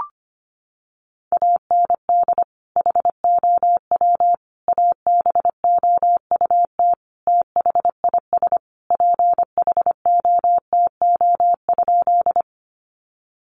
Morse code, heard as a keyed tone, about 25 wpm, 700 Hz. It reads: AND HOW ABOUT THIS PHOTO?
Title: Sentences from Top 1200 Words 25wpm